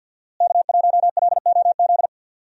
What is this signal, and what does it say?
Morse code, heard as a keyed tone, about 48 wpm, 700 Hz. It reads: K1LOZ
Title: USA Call Signs - 5 Characters - Repeated 3x - 50wpm